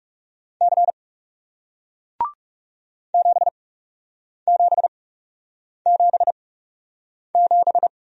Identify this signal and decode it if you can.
Morse code, heard as a keyed tone, about 40 wpm, spaced as for 8 wpm, 700 Hz. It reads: C 7777